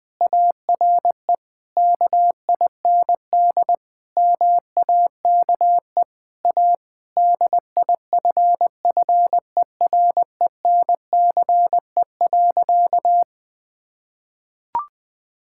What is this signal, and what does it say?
Morse code, heard as a keyed tone, about 20 wpm, 700 Hz. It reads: ARE KIND MAKE A DIFFERENCE.
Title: Sentences with 3-Word Phrases - Spoken Beginning - 20wpm